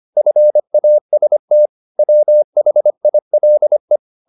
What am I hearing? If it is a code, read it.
Morse code, heard as a keyed tone, about 25 wpm, 600 Hz. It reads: FAST WHILE